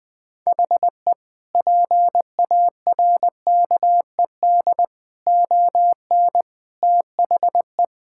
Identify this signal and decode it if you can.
Morse code, heard as a keyed tone, about 21 wpm, 700 Hz. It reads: HE PARKED ON THE
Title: Sentences with 4-Word Phrases - 20wpm